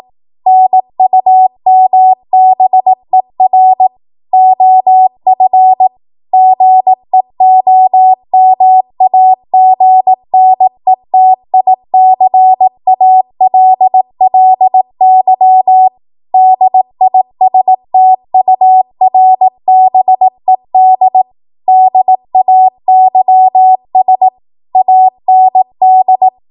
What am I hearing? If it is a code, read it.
Morse code, heard as a keyed tone, about 18 wpm, 750 Hz. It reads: NUMBER OF GEOMAGNETICALLY DISTURBED DAYS AND
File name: Morse / 18WPM